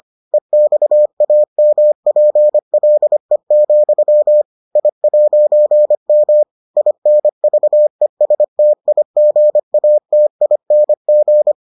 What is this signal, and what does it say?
Morse code, heard as a keyed tone, about 25 wpm, 600 Hz. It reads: EXAMPLE, I'M INVESTIGATING